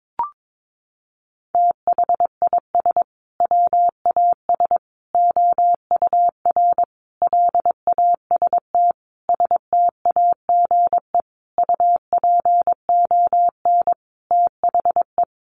Morse code, heard as a keyed tone, about 22 wpm, 700 Hz. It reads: THIS WAS OUR LAST STAGE UPON THE
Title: Sentences from Top 1200 Words 22wpm